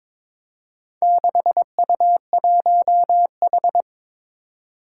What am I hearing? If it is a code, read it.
Morse code, heard as a keyed tone, about 22 wpm, 700 Hz. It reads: 6U1H
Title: Four Challenging Letter-Number Combinations 22wpm